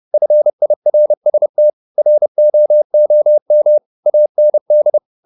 Morse code, heard as a keyed tone, about 30 wpm, 600 Hz. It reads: FIRST ROOM AND
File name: the old man and the sea / the old man and the sea 1 30wpm